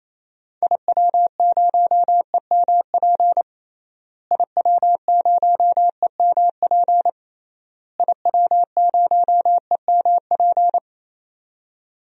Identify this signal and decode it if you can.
Morse code, heard as a keyed tone, about 28 wpm, 700 Hz. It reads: IW0EMP IW0EMP IW0EMP
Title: DXCC-100 Call Signs - 6 Characters - Repeated 3x - 28wpm